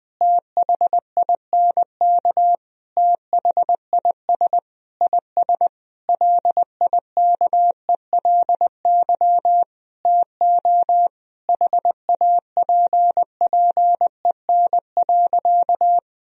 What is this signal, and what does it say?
Morse code, heard as a keyed tone, about 20 wpm, 700 Hz. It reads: THINK THIS IS LIKELY TO HAPPEN.
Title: Sentences from Top 1000 Words - Review Part 4 - 20wpm